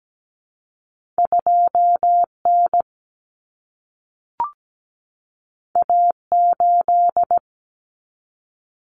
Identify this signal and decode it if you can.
Morse code, heard as a keyed tone, about 17 wpm, 700 Hz. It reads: 2N A8
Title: Common Cut Numbers - 2-Digit 17wpm